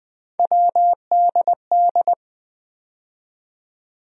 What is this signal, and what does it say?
Morse code, heard as a keyed tone, about 20 wpm, 700 Hz. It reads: WDD
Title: USA Call Sign Postfixes - 3 Letters 20wpm